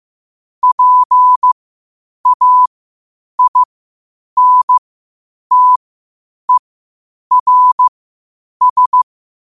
Morse code, heard as a keyed tone, about 15 wpm, spaced as for 8 wpm, 1000 Hz. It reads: PAINTERS
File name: qotdfiles / 8wpm